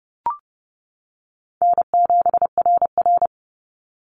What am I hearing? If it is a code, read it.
Morse code, heard as a keyed tone, about 30 wpm, 700 Hz. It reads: N7RR
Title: NA SOTA Activator Callsigns 30wpm